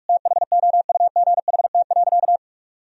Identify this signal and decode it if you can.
Morse code, heard as a keyed tone, about 46 wpm, 700 Hz. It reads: THOUGHT.
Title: Sentences from Top 1000 Words - Review Part 3 - 45wpm